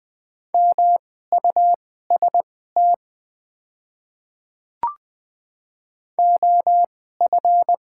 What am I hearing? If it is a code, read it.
Morse code, heard as a keyed tone, about 20 wpm, spaced as for 14 wpm, 690 Hz. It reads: MUST OF
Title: 2x Character Spacing - Top 500 Words - 20wpm